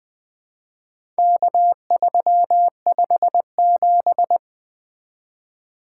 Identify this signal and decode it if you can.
Morse code, heard as a keyed tone, about 20 wpm, 700 Hz. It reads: K357
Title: Four Letter-Number Combinations - 20wpm